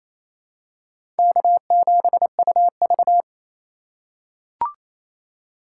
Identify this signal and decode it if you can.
Morse code, heard as a keyed tone, about 28 wpm, 700 Hz. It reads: K7UV
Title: USA Call Signs - 2x Character Spacing - 28wpm